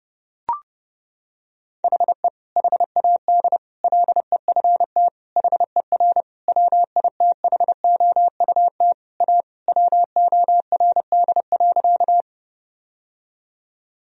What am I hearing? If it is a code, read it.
Morse code, heard as a keyed tone, about 30 wpm, 700 Hz. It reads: HE HAD LEFT HER WITHOUT A WORD.
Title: Sentences from Top 1000 Words - Review Part 3 - 30wpm